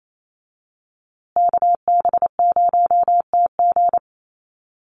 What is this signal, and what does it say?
Morse code, heard as a keyed tone, about 28 wpm, 700 Hz. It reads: KB0TG
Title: USA Call Signs Repeated 3x 28wpm